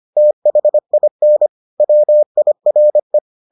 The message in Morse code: THIN WIRE

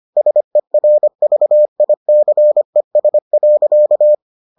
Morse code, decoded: SERVICES.

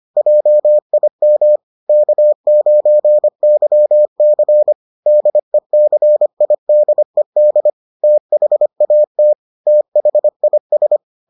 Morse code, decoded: JIM K9YC DECIDED THAT THIS

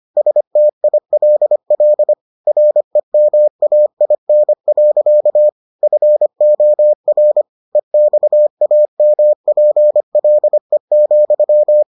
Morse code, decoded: STILL REMAIN. FOR EXAMPLE,

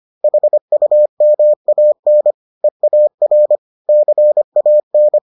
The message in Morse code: HUMAN EAR CAN

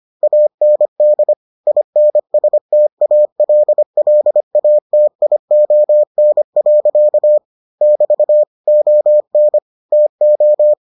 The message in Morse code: AND INSTALLATION. <BT> ON TO

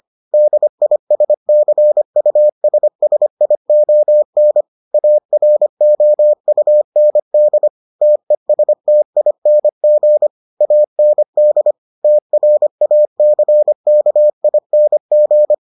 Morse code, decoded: DISCUSSION AROUND TESTING AND TRACKING

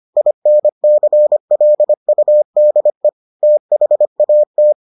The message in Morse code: INCLUDE THAT

